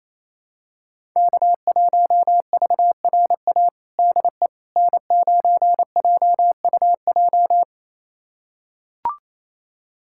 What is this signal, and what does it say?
Morse code, heard as a keyed tone, about 28 wpm, 700 Hz. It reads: K1VRA DE N9JUJ
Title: USA Callsign DE Callsign - 5 Characters - 28wpm